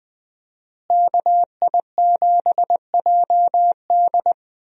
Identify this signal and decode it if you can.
Morse code, heard as a keyed tone, about 20 wpm, 700 Hz. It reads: KI7JD
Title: USA Call Signs - 5 Characters - Repeated 2x - 20wpm